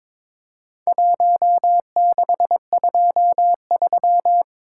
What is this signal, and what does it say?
Morse code, heard as a keyed tone, about 22 wpm, 700 Hz. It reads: 1623
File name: practice / 22wpm / Four Letter-Number Combinations - 22wpm